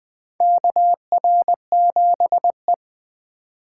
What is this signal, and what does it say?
Morse code, heard as a keyed tone, about 20 wpm, 700 Hz. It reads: KR7E
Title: USA Call Signs - 4 Characters - 2x Character Spacing - 20wpm